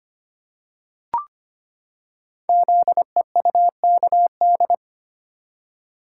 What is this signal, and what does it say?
Morse code, heard as a keyed tone, about 25 wpm, 700 Hz. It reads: ZEUKD
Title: Code Groups - 5 Letters - 25wpm